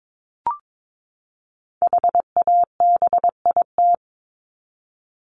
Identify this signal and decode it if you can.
Morse code, heard as a keyed tone, about 22 wpm, 700 Hz. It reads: HABIT
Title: Top 1751-2000 Words 22wpm